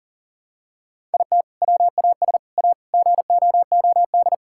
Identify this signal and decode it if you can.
Morse code, heard as a keyed tone, about 40 wpm, 700 Hz. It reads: IT WAS A GOOD
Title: Sentences from Top 1000 Words - Review Part 5 - 40wpm